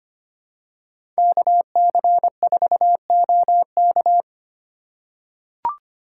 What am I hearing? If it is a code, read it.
Morse code, heard as a keyed tone, about 25 wpm, 700 Hz. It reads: KC4OK
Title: USA Call Signs - 5 Characters - 4x Character Spacing - 25wpm